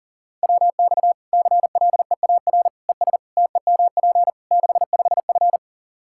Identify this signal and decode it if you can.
Morse code, heard as a keyed tone, about 40 wpm, 700 Hz. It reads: WX CLEAR ES TEMP 65F
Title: Practice QSO - Exchange 2 - Info - 40wpm